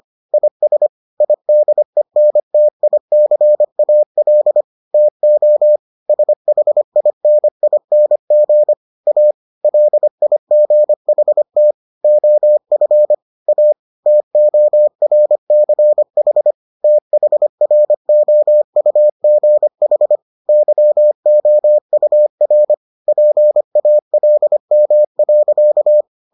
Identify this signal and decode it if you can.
Morse code, heard as a keyed tone, about 25 wpm, 600 Hz. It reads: IS IDENTICAL TO SHINING A LIGHT OF A TORCH THROUGH YOUR PALM.